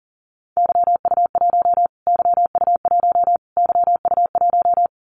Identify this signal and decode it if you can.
Morse code, heard as a keyed tone, about 40 wpm, 700 Hz. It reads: YU1 YU1 YU1
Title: Callsign Prefixes for Top 25 DXCC - Repeated 3x - 40wpm